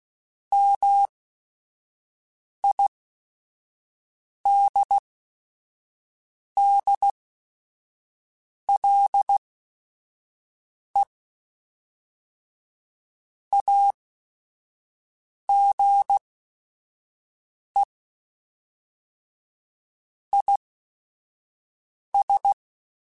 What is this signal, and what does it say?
Morse code, heard as a keyed tone, about 16 wpm, spaced as for 5 wpm, 800 Hz. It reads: MIDDLE AGE IS
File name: Chuckles / Chuckles-mp3s / MiddleAge @5